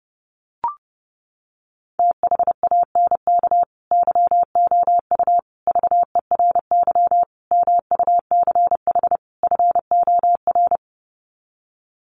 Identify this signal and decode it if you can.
Morse code, heard as a keyed tone, about 30 wpm, 700 Hz. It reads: THANK YOU VERY MUCH FOR
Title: Sets of 5 Words 30wpm